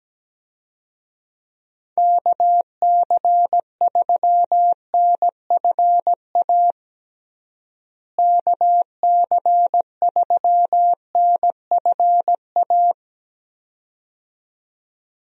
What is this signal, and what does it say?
Morse code, heard as a keyed tone, about 17 wpm, 700 Hz. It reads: KC3NFA KC3NFA
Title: USA Call Signs - 6 Characters - Repeated 2x - 17wpm